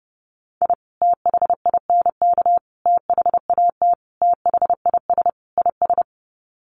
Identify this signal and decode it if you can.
Morse code, heard as a keyed tone, about 28 wpm, 700 Hz. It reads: I THINK THAT THIS IS
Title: Sets of 5 Words Encore2 30wpm